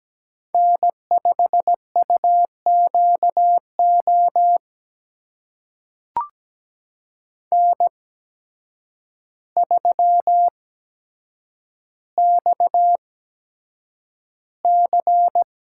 Morse code, heard as a keyed tone, about 17 wpm, spaced as for 8 wpm, 700 Hz. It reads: N5UQO N3XC
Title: USA Call Signs - 5 Characters - 8x Character Spacing - 17wpm